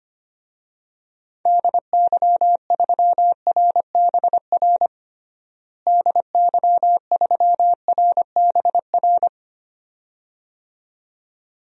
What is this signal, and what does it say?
Morse code, heard as a keyed tone, about 25 wpm, 700 Hz. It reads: DY3RBR DY3RBR
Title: DXCC-100 Call Signs - 6 Characters - Repeated 2x - 25wpm